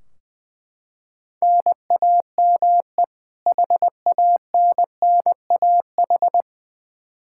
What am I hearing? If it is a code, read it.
Morse code, heard as a keyed tone, about 20 wpm, 700 Hz. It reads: NAME HANNAH
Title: Practice QSO - Exchange 1 - Beginning - 20wpm